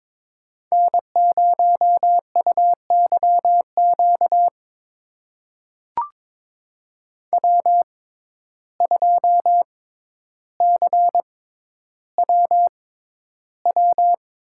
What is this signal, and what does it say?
Morse code, heard as a keyed tone, about 22 wpm, spaced as for 11 wpm, 700 Hz. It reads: N0UYQ W2CWW
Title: USA Call Signs - 5 Characters - 6x Character Spacing - 22wpm